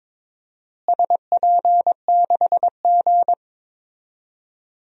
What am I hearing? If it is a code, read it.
Morse code, heard as a keyed tone, about 22 wpm, 700 Hz. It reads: SP6G
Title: DXCC-100 Call Signs - 4 Characters - Repeated 3x - 22wpm